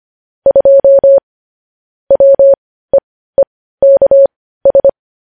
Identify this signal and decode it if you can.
Morse code, heard as a keyed tone, about 25 wpm, spaced as for 15 wpm, 550 Hz. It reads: 2 WEEKS
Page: news-headlines-in-morse-code-at-15-wpm